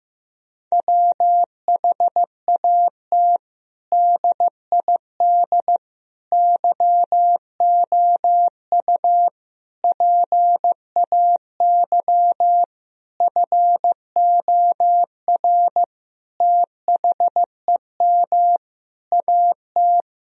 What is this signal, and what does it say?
Morse code, heard as a keyed tone, about 15 wpm, 700 Hz. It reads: WHAT DID YOU PAY FOR THEM AT
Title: Sentences from Top 1000 Words - Review Part 3 - 15wpm